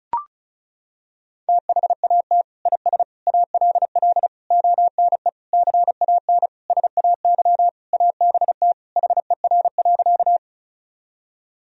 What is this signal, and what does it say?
Morse code, heard as a keyed tone, about 36 wpm, 700 Hz. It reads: THAT IS ALL ONE CAN SAY ABT HER.
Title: Sentences from Top 200 Words and Abbreviations 35wpm